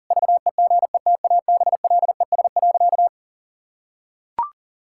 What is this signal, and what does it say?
Morse code, heard as a keyed tone, about 40 wpm, 700 Hz. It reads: VEGETABLES.
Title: Pangram Challenge v2 - 40wpm